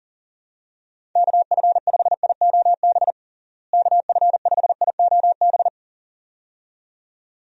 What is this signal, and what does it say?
Morse code, heard as a keyed tone, about 40 wpm, 700 Hz. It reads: KF5IOB KF5IOB
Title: USA Call Signs - 6 Characters - Repeated 2x - 40wpm